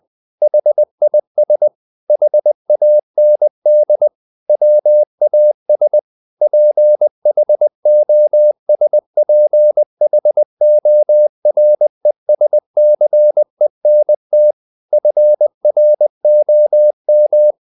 Morse code, heard as a keyed tone, about 20 wpm, 600 Hz. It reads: HIS HAND WAS PHOSPHORESCENT FROM
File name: the old man and the sea part 2 20wpm